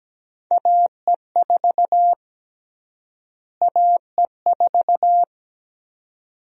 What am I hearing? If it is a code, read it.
Morse code, heard as a keyed tone, about 17 wpm, 700 Hz. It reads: AE4 AE4
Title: USA Call Sign Prefixes - Repeated 2x - 17wpm